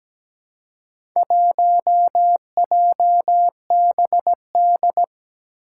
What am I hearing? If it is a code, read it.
Morse code, heard as a keyed tone, about 17 wpm, 700 Hz. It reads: 1JBD
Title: Four Challenging Letter-Number Combinations 17wpm